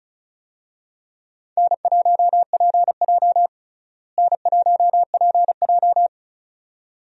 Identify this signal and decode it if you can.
Morse code, heard as a keyed tone, about 36 wpm, 700 Hz. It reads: N1PJ N1PJ
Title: USA Call Signs - 4 Characters - Repeated 2x - 35wpm